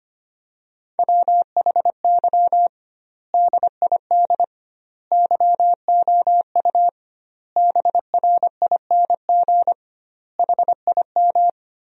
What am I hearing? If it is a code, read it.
Morse code, heard as a keyed tone, about 25 wpm, 700 Hz. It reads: WHY DID YOU BRING HIM
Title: 2x Word-Spacing - Sentences from Top 500 Words - 25wpm